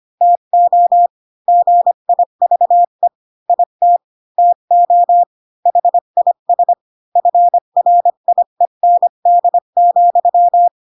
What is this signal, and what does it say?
Morse code, heard as a keyed tone, about 25 wpm, 700 Hz. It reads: TO GIVE IT TO HIS FRIEND,